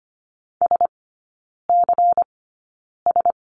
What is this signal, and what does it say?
Morse code, heard as a keyed tone, about 25 wpm, spaced as for 9 wpm, 700 Hz. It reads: SCS
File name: course / 25:10wpm / 100 Lesson 17e - Letter u - Three Characters 25wpm